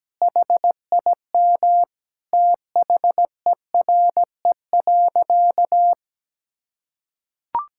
Morse code, heard as a keyed tone, about 17 wpm, 700 Hz. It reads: HIM THERE.
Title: Sentences from Top 1600 Words 17wpm